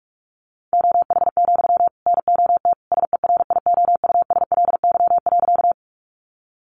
Morse code, heard as a keyed tone, about 44 wpm, 700 Hz. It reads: OH, NOT SERIOUSLY.
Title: Sentences from Top 1200 Words 45wpm